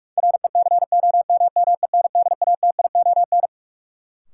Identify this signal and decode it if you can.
Morse code, heard as a keyed tone, about 42 wpm, 700 Hz. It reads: RECOMMENDATION
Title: Sentences from Top 1400 Words - New Word Drill - 45wpm